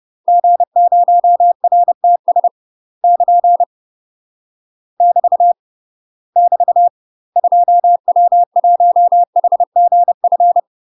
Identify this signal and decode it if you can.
Morse code, heard as a keyed tone, about 30 wpm, 700 Hz. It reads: G0RTS <KN> <BT> <BT> 2W1HGF
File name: UK MASTER QSOs TRAINING / 30WPM / UK MASTER QSOs 0002